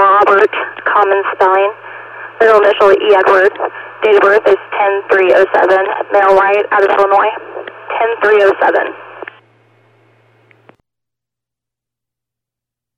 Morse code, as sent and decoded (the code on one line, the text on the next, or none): none